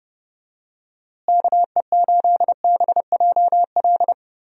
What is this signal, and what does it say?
Morse code, heard as a keyed tone, about 30 wpm, 700 Hz. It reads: KE8BJL
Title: USA Call Signs - 6 Characters - Repeated 2x - 30wpm